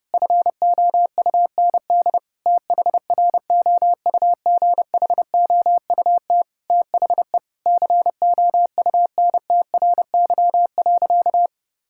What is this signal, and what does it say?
Morse code, heard as a keyed tone, about 32 wpm, 700 Hz. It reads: FOUND THROUGHOUT THE COUNTRY.
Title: Sentences from Top 1000 Words - Review Part 5 - 30wpm